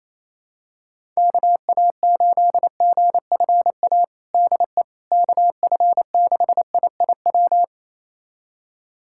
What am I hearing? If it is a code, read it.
Morse code, heard as a keyed tone, about 28 wpm, 700 Hz. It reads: KA8GFA DE KF6IIW
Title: USA Callsign DE Callsign 28wpm